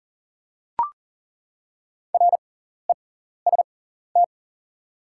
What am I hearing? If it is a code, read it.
Morse code, heard as a keyed tone, about 40 wpm, spaced as for 14 wpm, 690 Hz. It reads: REST
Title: 6x Character Spacing - Top 500 Words - 40wpm